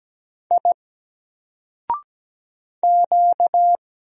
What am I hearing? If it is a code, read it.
Morse code, heard as a keyed tone, about 17 wpm, 700 Hz. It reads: I Q
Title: Single Character (Letter:Number:Punctuation:Prosign) Alt 17wpm